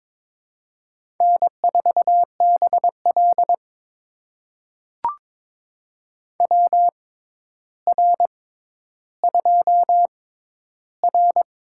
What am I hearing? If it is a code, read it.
Morse code, heard as a keyed tone, about 22 wpm, spaced as for 11 wpm, 700 Hz. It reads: N4BL WR2R
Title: USA Call Signs - 4 Characters - 6x Character Spacing - 22wpm